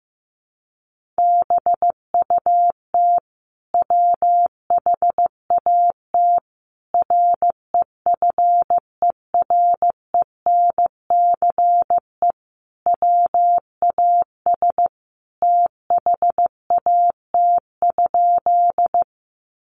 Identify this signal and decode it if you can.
Morse code, heard as a keyed tone, about 15 wpm, 700 Hz. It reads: BUT WHAT REFERENCE WAS THAT?
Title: Sentences from Top 1200 Words 15wpm